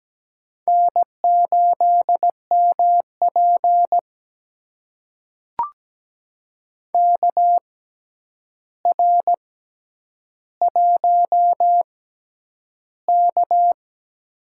Morse code, heard as a keyed tone, about 17 wpm, spaced as for 9 wpm, 700 Hz. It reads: N8MP KR1K
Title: USA Call Signs - 4 Characters - 6x Character Spacing - 17wpm